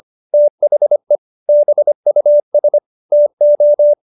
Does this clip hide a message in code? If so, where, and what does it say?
Yes: Morse code from 0.3 to 3.9 s, heard as a keyed tone, about 25 wpm, 590 Hz. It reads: THE BUS TO